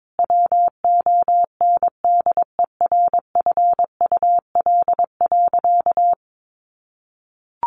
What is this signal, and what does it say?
Morse code, heard as a keyed tone, about 22 wpm, 700 Hz. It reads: WONDERFUL.
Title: Sentences from Top 1100 Words 22wpm